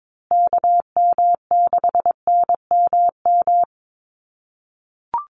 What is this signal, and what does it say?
Morse code, heard as a keyed tone, about 22 wpm, 700 Hz. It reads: KM6NMM
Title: USA Call Signs Repeated 2x 22wpm